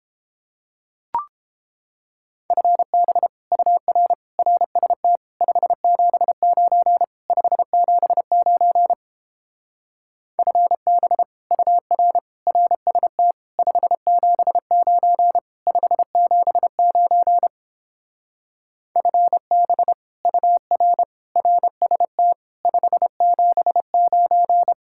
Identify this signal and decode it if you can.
Morse code, heard as a keyed tone, about 29 wpm, 700 Hz. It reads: FB UR RST 579 579 FB UR RST 579 579 FB UR RST 579